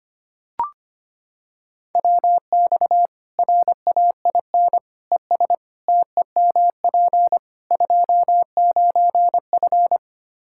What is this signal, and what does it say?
Morse code, heard as a keyed tone, about 25 wpm, 700 Hz. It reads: WX RAIN ES TEMP 29F
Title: Practice QSO - Exchange 2 - Info - 25wpm